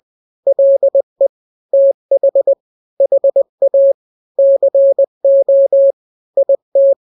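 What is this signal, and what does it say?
Morse code, heard as a keyed tone, about 20 wpm, 550 Hz. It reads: LE TH HA CO IT
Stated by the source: Morse drill of two-letter n-grams at 20/18 WPM (Farnsworth), from KK7RR.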